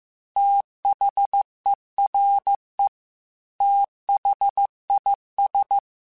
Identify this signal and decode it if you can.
Morse code, heard as a keyed tone, about 15 wpm, 800 Hz. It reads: THERE THIS
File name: QSOs / QSOs-mp3s / WD0BGP 2@13